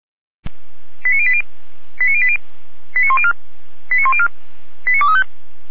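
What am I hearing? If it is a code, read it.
Five-tone selective call (ZVEI), heard as quick runs of five tones, tones of about 70 ms. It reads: 89080 89080 89104 89104 89235
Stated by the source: ZVEI five-tone selective call received by SDR.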